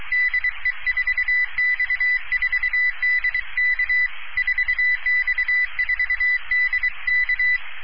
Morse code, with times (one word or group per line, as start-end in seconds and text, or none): 0.1-4.1 DE4X4DK
4.4-7.6 4X4DK